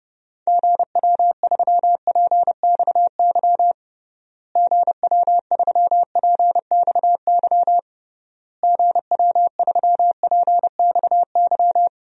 Morse code, heard as a keyed tone, about 31 wpm, 700 Hz. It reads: GW3PXY GW3PXY GW3PXY
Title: DXCC-100 Call Signs - 6 Characters - Repeated 3x - 30wpm